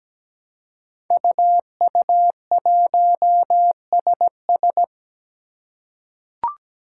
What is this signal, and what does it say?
Morse code, heard as a keyed tone, about 17 wpm, 700 Hz. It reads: UU1SS
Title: Five Challenging Letter-Number Combinations 17wpm